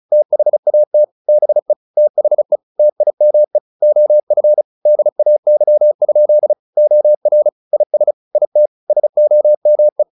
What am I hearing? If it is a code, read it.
Morse code, heard as a keyed tone, about 34 wpm, 600 Hz. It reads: THAT BE THE TIME OF DAY? OR IS IT SOME
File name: the old man and the sea / the old man and the sea 1 35wpm